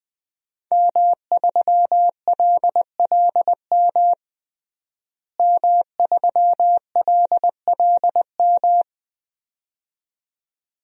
Morse code, heard as a keyed tone, about 21 wpm, 700 Hz. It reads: M3LLM M3LLM
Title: DXCC-100 Call Signs - 5 Characters - Repeated 2x - 20wpm